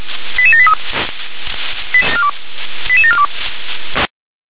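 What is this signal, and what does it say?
Five-tone selective call (ZVEI), heard as quick runs of five tones, tones of about 70 ms. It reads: 88783 88742
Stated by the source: ZVEI five-tone selective call received by SDR.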